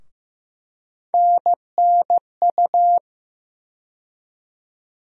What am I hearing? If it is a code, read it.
Morse code, heard as a keyed tone, about 15 wpm, 700 Hz. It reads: NNU
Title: USA Call Sign Postfixes - 3 Letters 15wpm